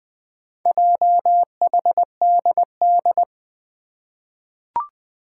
Morse code, heard as a keyed tone, about 20 wpm, 700 Hz. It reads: JHDD
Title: Four Challenging Letter-Number Combinations 20wpm